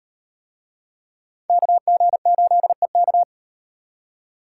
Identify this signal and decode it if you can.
Morse code, heard as a keyed tone, about 33 wpm, 700 Hz. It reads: KG8EK